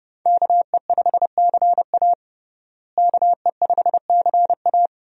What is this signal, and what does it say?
Morse code, heard as a keyed tone, about 31 wpm, 700 Hz. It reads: KE5CA KE5CA
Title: USA Call Signs - 5 Characters - Repeated 2x - 30wpm